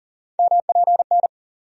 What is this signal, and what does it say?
Morse code, heard as a keyed tone, about 40 wpm, 700 Hz. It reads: MPN
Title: USA Call Sign Postfixes - 3 Letters 40wpm